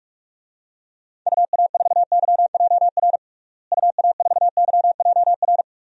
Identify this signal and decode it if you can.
Morse code, heard as a keyed tone, about 46 wpm, 700 Hz. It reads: UA4YJR UA4YJR
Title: DXCC-100 Call Signs - 6 Characters - Repeated 3x - 45wpm